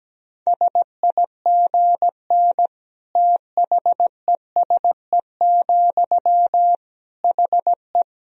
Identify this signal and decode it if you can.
Morse code, heard as a keyed tone, about 17 wpm, 700 Hz. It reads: SIGN THESE, HE
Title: Sentences from Top 1000 Words - Review Part 1 - 17wpm